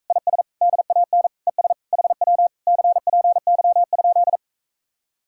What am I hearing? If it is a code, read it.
Morse code, heard as a keyed tone, about 45 wpm, 700 Hz. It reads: IS DAN ES HW CPY?